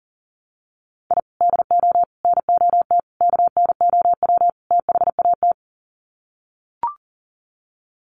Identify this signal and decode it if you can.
Morse code, heard as a keyed tone, about 40 wpm, 700 Hz. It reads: I DO NOT KNOW THAT
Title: Sets of 5 Words 40wpm